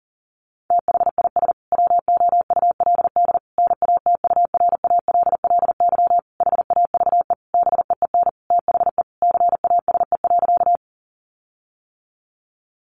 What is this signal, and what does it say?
Morse code, heard as a keyed tone, about 40 wpm, 700 Hz. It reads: THIS WOULD NATURALLY HAVE BEEN THE CASE.
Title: Sentences from Top 1300 Words 40wpm